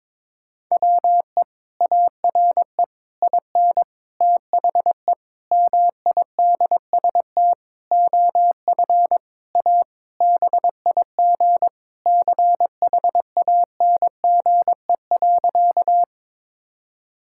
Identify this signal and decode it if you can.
Morse code, heard as a keyed tone, about 22 wpm, 700 Hz. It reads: WE ARE IN THE MIDST OF A BIG CHANGE.
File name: Sentences with 7-Word Phrases - 22wpm